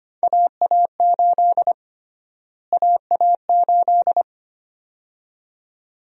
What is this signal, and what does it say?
Morse code, heard as a keyed tone, about 25 wpm, 700 Hz. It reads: AA8 AA8
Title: USA Call Sign Prefixes - Repeated 2x - 25wpm